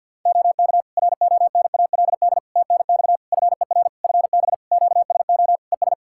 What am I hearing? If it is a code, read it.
Morse code, heard as a keyed tone, about 50 wpm, 700 Hz. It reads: OK RONALD TNX FER FB QSO ES
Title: Practice QSO - Exchange 4 - Ending - 50wpm